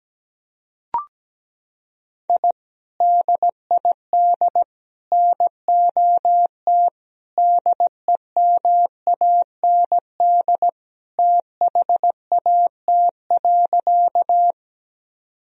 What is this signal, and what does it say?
Morse code, heard as a keyed tone, about 17 wpm, 700 Hz. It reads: I DID NOT DEMAND THAT.
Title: Sentences from Top 1000 Words - Review Part 4 - 17wpm